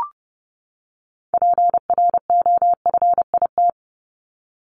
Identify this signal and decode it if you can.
Morse code, heard as a keyed tone, about 30 wpm, 700 Hz. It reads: PROFIT
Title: Top 751-1000 Words 30wpm